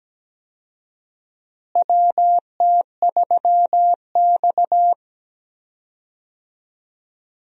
Morse code, heard as a keyed tone, about 17 wpm, 700 Hz. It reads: WT3X
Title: USA Call Signs - 4 Characters - Repeated 2x - 17wpm